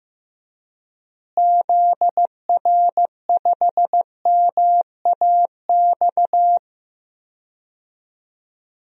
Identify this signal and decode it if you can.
Morse code, heard as a keyed tone, about 16 wpm, 700 Hz. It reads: ZR5MAX
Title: DXCC-100 Call Signs - 6 Characters - Repeated 3x - 15wpm